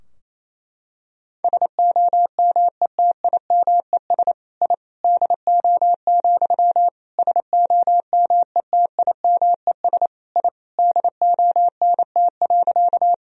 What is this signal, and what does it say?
Morse code, heard as a keyed tone, about 28 wpm, 700 Hz. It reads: SOMETIMES I DO, SOMETIMES I DONT.
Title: Sentences from Top 1000 Words - Review Part 3 - 28wpm